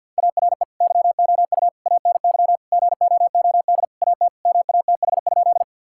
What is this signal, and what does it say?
Morse code, heard as a keyed tone, about 50 wpm, 700 Hz. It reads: ARE YOU ANY GOOD AT MATH?
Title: Sentences from Top 1000 Words - Review Part 6 - 50wpm